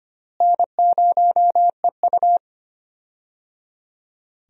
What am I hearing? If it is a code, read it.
Morse code, heard as a keyed tone, about 25 wpm, 700 Hz. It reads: N0EU
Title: USA Call Signs - 4 Characters - Repeated 3x - 25wpm